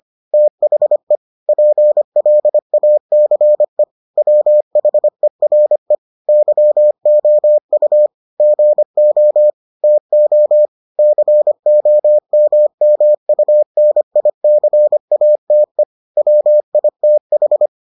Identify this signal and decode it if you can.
Morse code, heard as a keyed tone, about 25 wpm, 600 Hz. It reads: THE PLACE WHERE YOU GO TO COMMUNICATE WITH